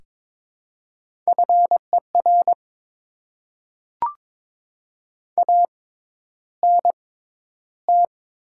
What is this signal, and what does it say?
Morse code, heard as a keyed tone, about 22 wpm, spaced as for 11 wpm, 700 Hz. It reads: FER ANT
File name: Top 20 QSO Elements - 6x Character Spacing - 22wpm